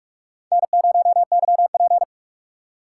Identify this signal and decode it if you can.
Morse code, heard as a keyed tone, about 41 wpm, 700 Hz. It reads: N0YP
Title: USA Call Signs - 2x Character Spacing - 45wpm